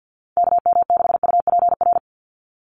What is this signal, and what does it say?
Morse code, heard as a keyed tone, about 50 wpm, 700 Hz. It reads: KM6UPR
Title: USA Call Signs 50wpm